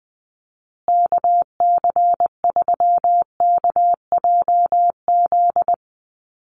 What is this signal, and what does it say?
Morse code, heard as a keyed tone, about 20 wpm, 700 Hz. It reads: KC3KJZ
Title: USA Call Signs Repeated 2x 20wpm